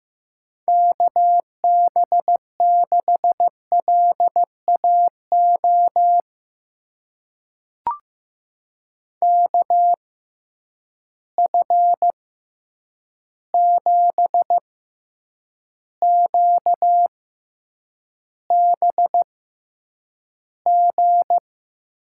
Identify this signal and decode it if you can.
Morse code, heard as a keyed tone, about 15 wpm, spaced as for 8 wpm, 700 Hz. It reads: KB6LAO KF7QBG
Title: USA Call Signs - 6 Characters - 6x Character Spacing - 15wpm